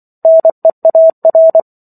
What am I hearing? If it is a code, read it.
Morse code, heard as a keyed tone, about 24 wpm, 650 Hz. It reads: NEAR